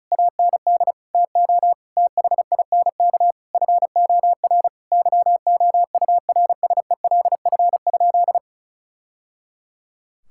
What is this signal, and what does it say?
Morse code, heard as a keyed tone, about 36 wpm, 700 Hz. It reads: AND TO THINK FOR YOURSELF?
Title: Sentences from Top 1000 Words - Review Part 6 - 35wpm